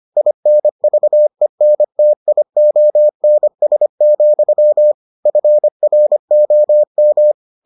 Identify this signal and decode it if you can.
Morse code, heard as a keyed tone, about 25 wpm, 600 Hz. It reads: INVENTIONS, FROM